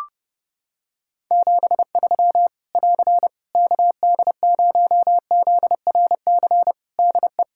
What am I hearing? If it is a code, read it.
Morse code, heard as a keyed tone, about 28 wpm, 700 Hz. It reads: 73 <AR> KD0ZRC DE
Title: Practice QSO - Exchange 4 - Ending - 30wpm